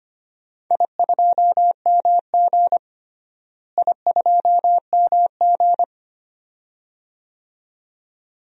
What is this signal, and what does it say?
Morse code, heard as a keyed tone, about 25 wpm, 700 Hz. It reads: I2MG I2MG
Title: DXCC-100 Call Signs - 4 Characters - Repeated 2x - 25wpm